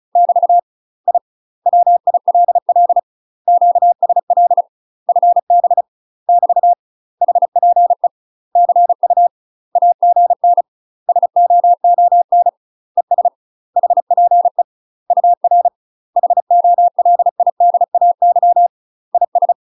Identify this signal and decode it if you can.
Morse code, heard as a keyed tone, about 35 wpm, 700 Hz. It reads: <BT> I WILL QSL FB <BT> HPE CU AGN SOON ES HPE UR HOLIDAY IS
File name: UK MASTER QSOs TRAINING / 35WPM / UK MASTER QSOs 0033